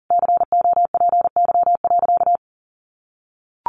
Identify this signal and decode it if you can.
Morse code, heard as a keyed tone, about 40 wpm, 700 Hz. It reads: COPY.
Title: Sentences from Top 1400 Words 40wpm